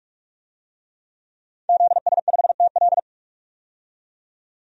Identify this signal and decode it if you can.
Morse code, heard as a keyed tone, about 47 wpm, 700 Hz. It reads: ZS5TL
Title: DXCC-100 Call Signs - 5 Characters - 45wpm